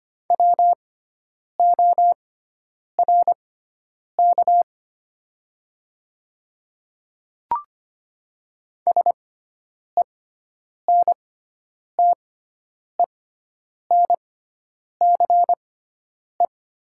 Morse code, heard as a keyed tone, about 25 wpm, spaced as for 9 wpm, 700 Hz. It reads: WORK SENTENCE
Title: 6x Character Spacing - Top 300 Words - 25wpm